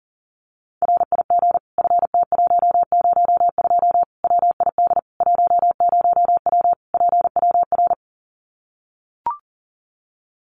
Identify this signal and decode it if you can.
Morse code, heard as a keyed tone, about 40 wpm, 700 Hz. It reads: RIG FT102 WID 10W PWR
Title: Common QSO Phrases 40wpm